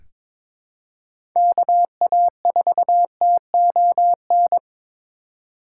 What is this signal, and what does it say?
Morse code, heard as a keyed tone, about 22 wpm, 700 Hz. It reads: KA4TON
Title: USA Call Signs - 6 Characters - 2x Character Spacing - 22wpm